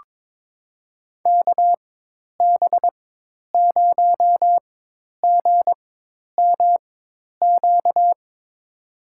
Morse code, heard as a keyed tone, about 21 wpm, spaced as for 10 wpm, 700 Hz. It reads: KB0GMQ
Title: USA Call Signs - 4x Character Spacing - 22wpm